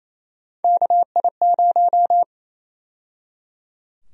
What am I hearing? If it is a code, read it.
Morse code, heard as a keyed tone, about 28 wpm, 700 Hz. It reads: KI0
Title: USA Call Sign Prefixes - 28wpm